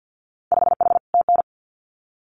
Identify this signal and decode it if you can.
Morse code, heard as a keyed tone, about 50 wpm, 700 Hz. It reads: 5H TN
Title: Winter Field Day Exchanges Encore 50wpm